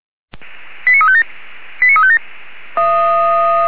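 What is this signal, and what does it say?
Five-tone selective call (ZVEI), heard as quick runs of five tones, tones of about 70 ms. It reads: 89357 89357
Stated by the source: ZVEI five-tone selective call received with an SDR.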